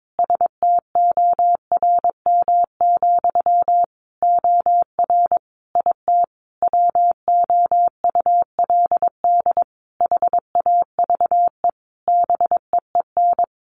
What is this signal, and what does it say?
Morse code, heard as a keyed tone, about 22 wpm, 700 Hz. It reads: STORM, OR IT WOULD HAVE BEEN
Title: Sentences from Top 1500 Words 22wpm